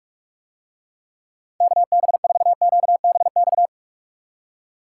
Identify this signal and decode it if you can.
Morse code, heard as a keyed tone, about 46 wpm, 700 Hz. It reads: KB4QBX